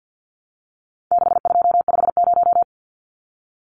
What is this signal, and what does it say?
Morse code, heard as a keyed tone, about 50 wpm, 700 Hz. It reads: 6250
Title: Numbers - 4-Digit 50wpm